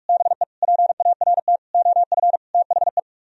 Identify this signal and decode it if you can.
Morse code, heard as a keyed tone, about 49 wpm, 700 Hz. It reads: BE PART OF THE